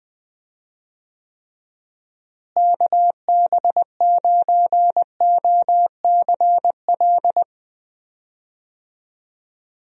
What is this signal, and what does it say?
Morse code, heard as a keyed tone, about 21 wpm, 700 Hz. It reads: KB9OCL
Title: USA Call Signs - 6 Characters - Repeated 3x - 20wpm